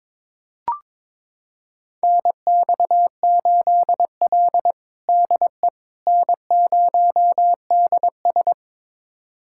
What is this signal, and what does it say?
Morse code, heard as a keyed tone, about 22 wpm, 700 Hz. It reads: NX8L DE N0DS
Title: USA Callsign DE Callsign - 4 Characters - 22wpm